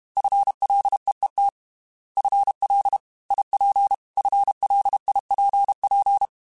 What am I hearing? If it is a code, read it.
Morse code, heard as a keyed tone, about 32 wpm, 800 Hz. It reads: FLEET FL IP FLIPP
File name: FL / FL mp3s / Fast FL @30